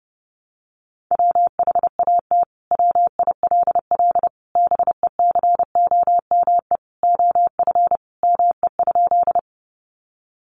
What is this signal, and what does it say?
Morse code, heard as a keyed tone, about 28 wpm, 700 Hz. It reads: WHAT WILL BECOME OF ME?
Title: Sentences from Top 600 Words 30wpm